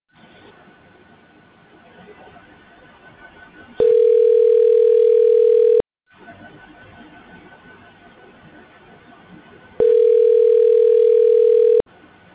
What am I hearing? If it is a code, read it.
Ringback tone (North America), 440 + 480 Hz.